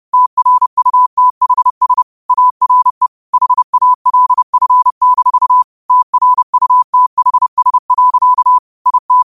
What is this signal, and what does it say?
Morse code, heard as a keyed tone, about 30 wpm, 1000 Hz. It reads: TRUTHS ARE HALF-TRUTHS. IT